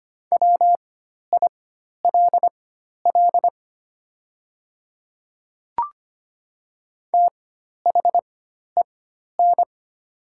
Morse code, heard as a keyed tone, about 25 wpm, spaced as for 12 wpm, 700 Hz. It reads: WILL THEN
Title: 4x Character Spacing - Top 200 Words - 25wpm